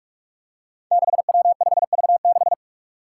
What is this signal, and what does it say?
Morse code, heard as a keyed tone, about 44 wpm, 700 Hz. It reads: 6W5V6
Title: Five Challenging Letter-Number Combinations 45wpm